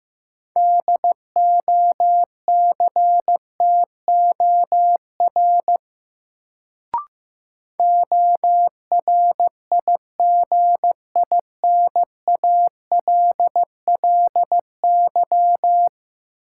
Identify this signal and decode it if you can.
Morse code, heard as a keyed tone, about 15 wpm, 700 Hz. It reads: DOCTOR ORIGINALLY
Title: Sentences from Top 1500 Words - New Word Drill - 15wpm